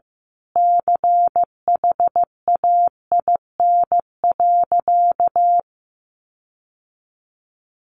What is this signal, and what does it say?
Morse code, heard as a keyed tone, about 15 wpm, 700 Hz. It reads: CHAIN.
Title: Sentences from Top 1500 Words 15wpm